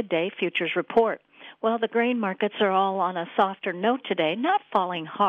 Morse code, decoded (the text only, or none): none